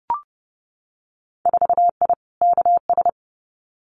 Morse code, heard as a keyed tone, about 30 wpm, 700 Hz. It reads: 4I KS
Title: Winter Field Day Exchanges Encore 30wpm